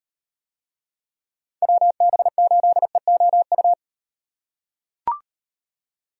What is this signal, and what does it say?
Morse code, heard as a keyed tone, about 37 wpm, 700 Hz. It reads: WB8EOU